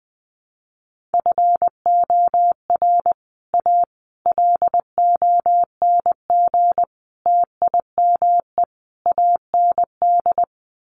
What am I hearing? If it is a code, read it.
Morse code, heard as a keyed tone, about 20 wpm, 700 Hz. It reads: FOR A LONG TIME AND
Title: Sets of 5 Words Encore2 20wpm